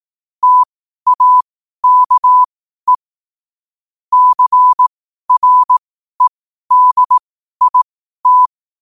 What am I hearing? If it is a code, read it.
Morse code, heard as a keyed tone, about 18 wpm, spaced as for 12 wpm, 1000 Hz. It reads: TAKE CREDIT
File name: qotdfiles / 12wpm